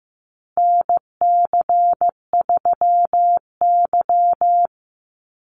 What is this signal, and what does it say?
Morse code, heard as a keyed tone, about 15 wpm, 700 Hz. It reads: NC3Y
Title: Call Signs Repeated 3x - 15wpm